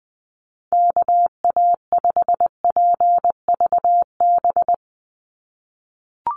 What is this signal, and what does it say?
Morse code, heard as a keyed tone, about 20 wpm, 700 Hz. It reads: KA5PVB
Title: NA SOTA Chaser Callsigns 20wpm